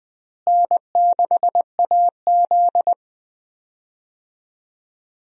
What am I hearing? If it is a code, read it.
Morse code, heard as a keyed tone, about 20 wpm, 700 Hz. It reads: N6AZ
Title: USA Call Signs - 4 Characters - Repeated 2x - 20wpm